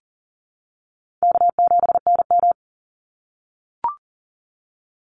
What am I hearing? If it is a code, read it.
Morse code, heard as a keyed tone, about 40 wpm, 700 Hz. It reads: K7NM
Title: Easy Call Signs - 40wpm